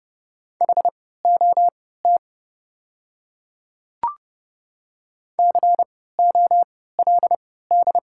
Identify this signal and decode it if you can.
Morse code, heard as a keyed tone, about 30 wpm, spaced as for 17 wpm, 690 Hz. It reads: HOT COLD
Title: 3x Character Spacing - Top 500 Words - 30wpm